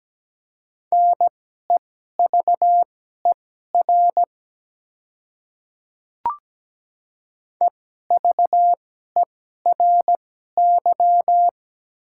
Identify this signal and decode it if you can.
Morse code, heard as a keyed tone, about 17 wpm, spaced as for 12 wpm, 700 Hz. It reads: NEVER EVERY
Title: 2x Character Spacing - Top 300 Words - 17wpm